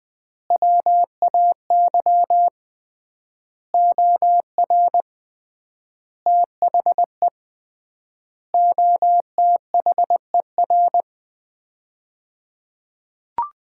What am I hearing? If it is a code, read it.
Morse code, heard as a keyed tone, about 20 wpm, 700 Hz. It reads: WAY OR THE OTHER
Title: 3x Word-Spacing - Sets of 5 Words - 20wpm